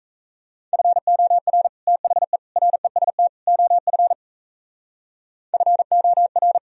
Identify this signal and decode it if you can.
Morse code, heard as a keyed tone, about 35 wpm, 700 Hz. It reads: FOR THE REST OF FOR
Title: Speed-Racing - Sets of 4 Words 28wpm